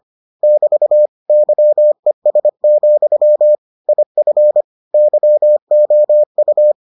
Morse code, heard as a keyed tone, about 25 wpm, 600 Hz. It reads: <BT> YES, IF YOU